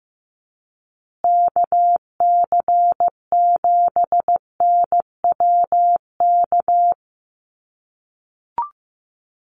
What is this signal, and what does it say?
Morse code, heard as a keyed tone, about 15 wpm, 700 Hz. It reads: KC7NWK
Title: USA Call Signs Repeated 3x 15wpm